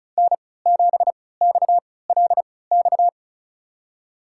Morse code, heard as a keyed tone, about 36 wpm, spaced as for 20 wpm, 700 Hz. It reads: N7XLX
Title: USA Call Signs - 5 Characters - 3x Character Spacing - 35wpm